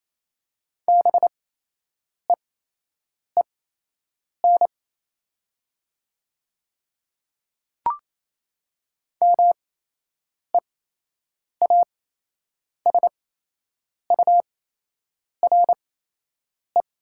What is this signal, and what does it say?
Morse code, heard as a keyed tone, about 28 wpm, spaced as for 8 wpm, 700 Hz. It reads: BEEN MEASURE
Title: 8x Character Spacing - Top 500 Words - 28wpm